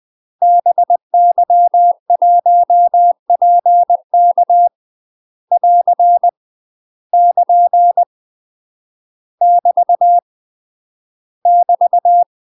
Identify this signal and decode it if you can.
Morse code, heard as a keyed tone, about 20 wpm, 700 Hz. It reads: BY1PK <AR> <KN> <BT> <BT>